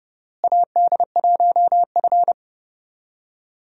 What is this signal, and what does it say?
Morse code, heard as a keyed tone, about 32 wpm, 700 Hz. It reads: AD1F
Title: USA Call Signs - 4 Characters - Repeated 3x - 30wpm